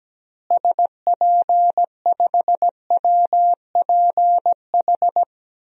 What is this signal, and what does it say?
Morse code, heard as a keyed tone, about 17 wpm, 700 Hz. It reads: SP5WPH
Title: DXCC-100 Call Signs - 6 Characters - Repeated 3x - 17wpm